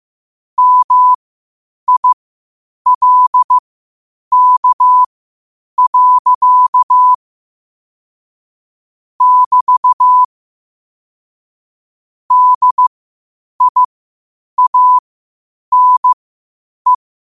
Morse code, heard as a keyed tone, about 15 wpm, spaced as for 8 wpm, 1000 Hz. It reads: MILK. <BT> DIANE
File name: qotdfiles / 8wpm